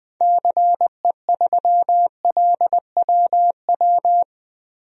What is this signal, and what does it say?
Morse code, heard as a keyed tone, about 21 wpm, 700 Hz. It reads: CE3LWW
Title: DXCC-100 Call Signs - 6 Characters - Repeated 3x - 20wpm